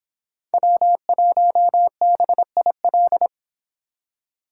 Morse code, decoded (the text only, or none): W1BIL